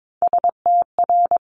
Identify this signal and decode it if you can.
Morse code, heard as a keyed tone, about 22 wpm, 700 Hz. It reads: STR
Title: Multi-letter Phonemes 22wpm